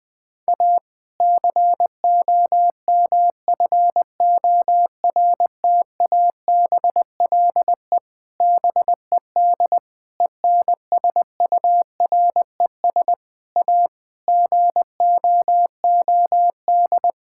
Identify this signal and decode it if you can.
Morse code, heard as a keyed tone, about 20 wpm, 700 Hz. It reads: A COMFORTABLE BED ENSURES A GOOD
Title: Difficult Sentences with Hints - 20wpm